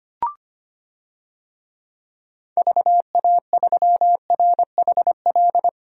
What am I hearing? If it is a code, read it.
Morse code, heard as a keyed tone, about 25 wpm, 700 Hz. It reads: VA3RHL
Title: DXCC-100 Call Signs - 6 Characters - 25wpm